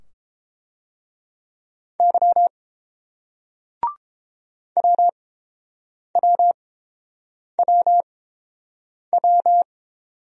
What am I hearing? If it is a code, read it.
Morse code, heard as a keyed tone, about 30 wpm, spaced as for 7 wpm, 700 Hz. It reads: Y WWWW